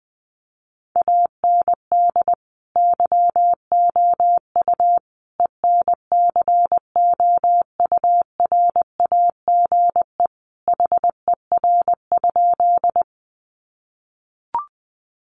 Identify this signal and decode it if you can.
Morse code, heard as a keyed tone, about 20 wpm, 700 Hz. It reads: AND YOU ENCOURAGE HER?
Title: Sentences from Top 1400 Words 20wpm